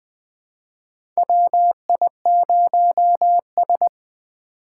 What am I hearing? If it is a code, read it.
Morse code, heard as a keyed tone, about 20 wpm, 700 Hz. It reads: WI0S
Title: 2023 POTA Activator Call Signs - 20wpm